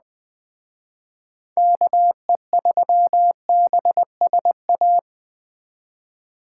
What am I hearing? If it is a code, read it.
Morse code, heard as a keyed tone, about 20 wpm, 700 Hz. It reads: KE3BSA
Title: Speed Racing - USA 2x3 Call Signs 17wpm